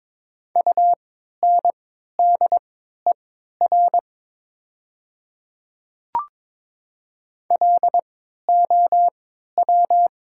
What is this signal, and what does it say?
Morse code, heard as a keyed tone, about 22 wpm, spaced as for 13 wpm, 700 Hz. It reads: UNDER LOW
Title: 3x Character Spacing - Top 200 Words - 22wpm